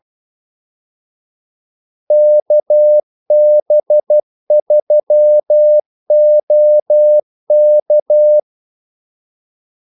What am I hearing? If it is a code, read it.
Morse code, heard as a keyed tone, about 12 wpm, 600 Hz. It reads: KB3OK
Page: 2 ekstra ordmellemrum mellem kaldesignalerne